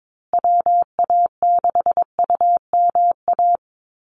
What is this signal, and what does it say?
Morse code, heard as a keyed tone, about 22 wpm, 700 Hz. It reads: WA6UMA
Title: USA Call Signs Repeated 3x 22wpm